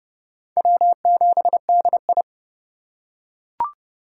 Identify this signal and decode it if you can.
Morse code, heard as a keyed tone, about 30 wpm, 700 Hz. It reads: W7DI
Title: USA Call Signs - 4 Characters - 3x Character Spacing - 30wpm